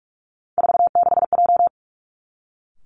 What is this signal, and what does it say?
Morse code, heard as a keyed tone, about 46 wpm, 700 Hz. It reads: 46J
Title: Three Challenging Letter-Number Combinations 45wpm